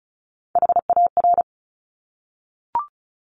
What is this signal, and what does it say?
Morse code, heard as a keyed tone, about 36 wpm, 700 Hz. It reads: HAR
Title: Top Three Letter Combinations in Words 35wpm